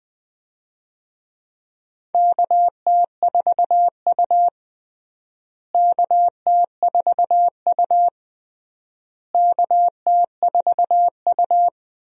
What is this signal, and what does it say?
Morse code, heard as a keyed tone, about 20 wpm, 700 Hz. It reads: KT4U KT4U KT4U
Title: USA Call Signs - 4 Characters - Repeated 3x - 20wpm